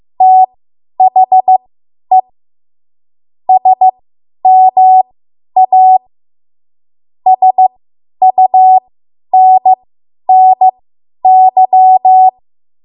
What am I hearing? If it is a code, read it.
Morse code, heard as a keyed tone, about 15 wpm, spaced as for 10 wpm, 750 Hz. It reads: THE SMA SUNNY
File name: Archive / 10 WPM / 10WPM